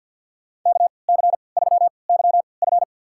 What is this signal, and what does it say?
Morse code, heard as a keyed tone, about 47 wpm, spaced as for 28 wpm, 700 Hz. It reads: KC3YF
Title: USA Call Signs - 5 Characters - 3x Character Spacing - 50wpm